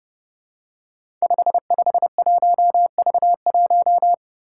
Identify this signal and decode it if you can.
Morse code, heard as a keyed tone, about 30 wpm, 700 Hz. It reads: 551V1
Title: Five Challenging Letter-Number Combinations 30wpm